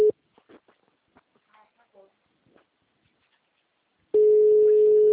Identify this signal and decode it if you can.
Ringback tone (Europe, 425 Hz), with speech.
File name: out- -2009- - - .37543